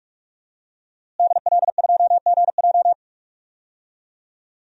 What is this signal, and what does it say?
Morse code, heard as a keyed tone, about 46 wpm, 700 Hz. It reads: DL2GJ
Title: DXCC-100 Call Signs - 5 Characters - Repeated 2x - 45wpm